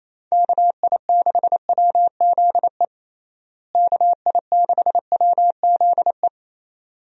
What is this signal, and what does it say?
Morse code, heard as a keyed tone, about 28 wpm, 700 Hz. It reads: KI6WZE KI6WZE
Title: USA Call Signs - 6 Characters - Repeated 2x - 28wpm